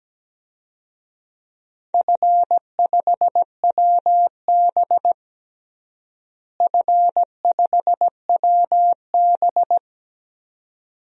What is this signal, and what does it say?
Morse code, heard as a keyed tone, about 17 wpm, 700 Hz. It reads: F5WB F5WB
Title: DXCC-100 Call Signs - 4 Characters - Repeated 2x - 17wpm